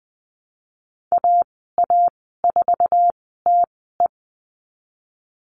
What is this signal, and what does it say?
Morse code, heard as a keyed tone, about 20 wpm, spaced as for 15 wpm, 700 Hz. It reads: AA4TE
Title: 062 Lesson 11g - Letter h - Call Signs 20wpm